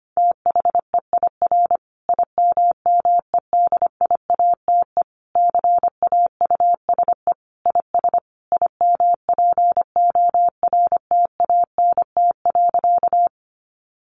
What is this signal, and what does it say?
Morse code, heard as a keyed tone, about 25 wpm, 700 Hz. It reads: THEIR IMMEDIATE CAUSE IS IMPORTANT.
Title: Sentences from Top 1200 Words 25wpm